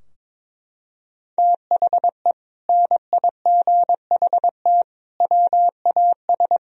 Morse code, heard as a keyed tone, about 22 wpm, 700 Hz. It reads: THE NIGHT WAS
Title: Sentences from Top 1000 Words - Review Part 2 - 22wpm